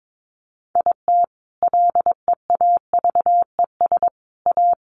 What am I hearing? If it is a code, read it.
Morse code, heard as a keyed tone, about 22 wpm, 700 Hz. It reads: IT LEAVES A
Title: A Sound Of Thunder 22wpm